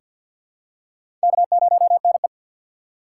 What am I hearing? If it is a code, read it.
Morse code, heard as a keyed tone, about 46 wpm, 700 Hz. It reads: K0NE
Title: USA Call Signs - 6x Character Spacing - 50wpm